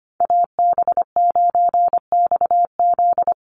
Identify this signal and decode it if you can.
Morse code, heard as a keyed tone, about 25 wpm, 700 Hz. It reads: AB9XZ
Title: USA Call Signs 25wpm